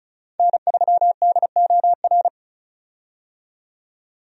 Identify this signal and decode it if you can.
Morse code, heard as a keyed tone, about 34 wpm, 700 Hz. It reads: N3DOR
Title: USA Call Signs - 5 Characters - 35wpm